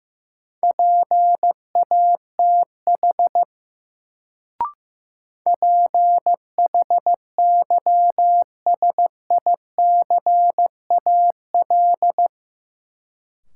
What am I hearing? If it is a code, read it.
Morse code, heard as a keyed tone, about 15 wpm, 700 Hz. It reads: PATH PHYSICAL
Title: Sentences from Top 1400 Words - New Word Drill Alt - 15wpm